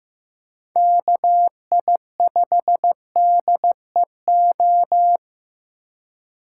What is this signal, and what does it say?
Morse code, heard as a keyed tone, about 18 wpm, 700 Hz. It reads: KI5DEO
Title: Speed Racing - USA 2x3 Call Signs 15wpm